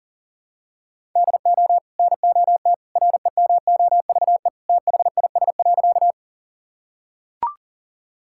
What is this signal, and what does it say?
Morse code, heard as a keyed tone, about 37 wpm, 700 Hz. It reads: DO NOT REMOVE THIS.
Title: Sentences from Top 1000 Words - Review Part 5 - 40wpm